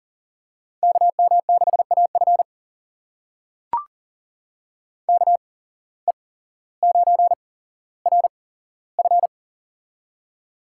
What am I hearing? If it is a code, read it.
Morse code, heard as a keyed tone, about 40 wpm, spaced as for 17 wpm, 700 Hz. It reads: KM6AF KE9RF
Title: USA Call Signs - 5 Characters - 8x Character Spacing - 40wpm